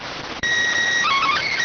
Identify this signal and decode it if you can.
SSTV (slow-scan television), PD 290.